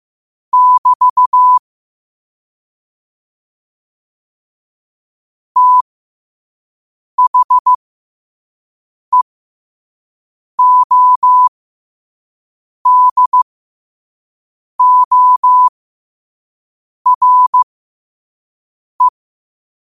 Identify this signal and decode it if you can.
Morse code, heard as a keyed tone, about 15 wpm, spaced as for 5 wpm, 1000 Hz. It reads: <BT> THEODORE
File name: qotdfiles / 5wpm